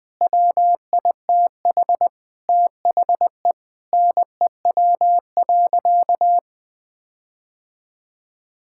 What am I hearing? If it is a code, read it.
Morse code, heard as a keyed tone, about 20 wpm, 700 Hz. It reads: WITH THE NEW.
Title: Sentences with 9-Word Phrases - 20wpm